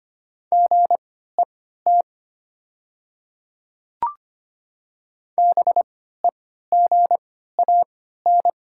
Morse code, heard as a keyed tone, about 25 wpm, spaced as for 14 wpm, 700 Hz. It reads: GET BEGAN
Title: 3x Character Spacing - Top 500 Words - 25wpm